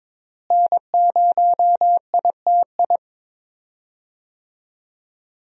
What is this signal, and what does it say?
Morse code, heard as a keyed tone, about 22 wpm, 700 Hz. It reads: N0ITI
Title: USA Call Signs - 5 Characters - Repeated 2x - 22wpm